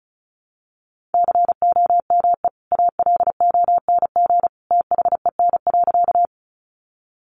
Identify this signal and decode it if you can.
Morse code, heard as a keyed tone, about 34 wpm, 700 Hz. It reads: COME ALONG THEN.
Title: Sentences from Top 700 Words 35wpm